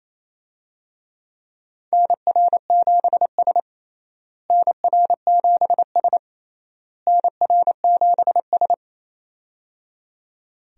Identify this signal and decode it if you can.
Morse code, heard as a keyed tone, about 27 wpm, 700 Hz. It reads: NR7S NR7S NR7S
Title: USA Call Signs - 4 Characters - Repeated 3x - 28wpm